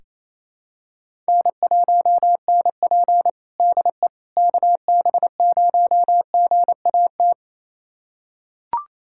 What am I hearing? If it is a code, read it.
Morse code, heard as a keyed tone, about 28 wpm, 700 Hz. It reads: N1NP DE KB0GAT
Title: USA Callsign DE Callsign 28wpm